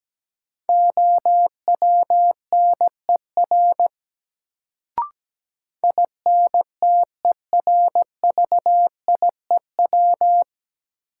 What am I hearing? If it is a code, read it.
Morse code, heard as a keyed tone, about 17 wpm, 700 Hz. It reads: OWNER INTERVIEW
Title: Sentences from Top 1600 Words - New Word Drill - 17wpm